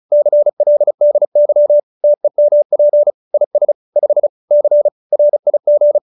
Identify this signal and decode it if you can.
Morse code, heard as a keyed tone, about 34 wpm, 590 Hz. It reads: CLDY TEMP IS 5 C RIG